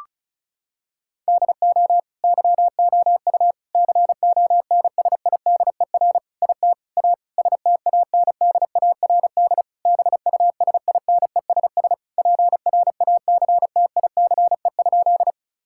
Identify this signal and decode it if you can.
Morse code, heard as a keyed tone, about 36 wpm, 700 Hz. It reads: DO YOU CONSIDER IT A STANDARD BUSINESS PRACTICE?
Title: Sentences from Top 1000 Words - Review Part 6 - 35wpm